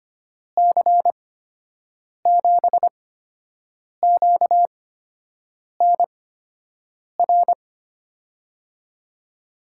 Morse code, heard as a keyed tone, about 25 wpm, spaced as for 7 wpm, 700 Hz. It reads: C7QNR
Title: USA Call Signs - 8x Character Spacing - 25wpm